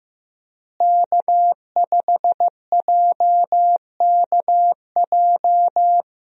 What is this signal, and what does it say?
Morse code, heard as a keyed tone, about 15 wpm, 700 Hz. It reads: K5JKJ
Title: USA Call Signs - 5 Characters - Repeated 3x - 15wpm